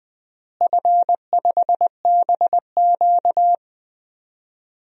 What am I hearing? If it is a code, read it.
Morse code, heard as a keyed tone, about 21 wpm, 700 Hz. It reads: F5BQ
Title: DXCC-100 Call Signs - 4 Characters - Repeated 2x - 20wpm